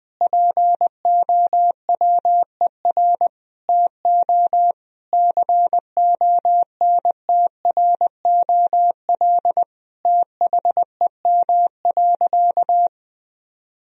Morse code, heard as a keyed tone, about 20 wpm, 700 Hz. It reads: POWER TO CONTROL THEM.
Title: Sentences from Top 1000 Words - Review Part 2 - 20wpm